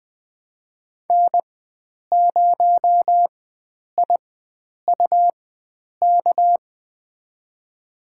Morse code, heard as a keyed tone, about 20 wpm, spaced as for 9 wpm, 700 Hz. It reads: N0IUK
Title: USA Call Signs - 4x Character Spacing - 20wpm